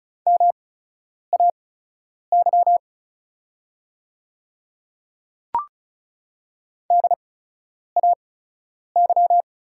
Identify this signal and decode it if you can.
Morse code, heard as a keyed tone, about 36 wpm, spaced as for 10 wpm, 700 Hz. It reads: MAY DAY
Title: 8x Character Spacing - Top 100 Words - 35wpm